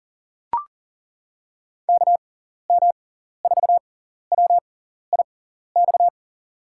Morse code, heard as a keyed tone, about 37 wpm, spaced as for 14 wpm, 700 Hz. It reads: KM4WIX
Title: USA Call Signs - 6 Characters - 6x Character Spacing - 40wpm